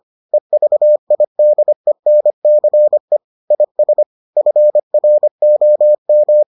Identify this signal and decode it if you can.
Morse code, heard as a keyed tone, about 25 wpm, 600 Hz. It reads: EVIDENCE IS FROM